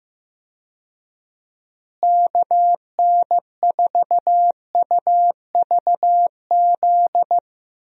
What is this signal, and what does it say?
Morse code, heard as a keyed tone, about 15 wpm, 700 Hz. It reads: KN4UVZ
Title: USA Call Signs - 6 Characters - Repeated 2x - 15wpm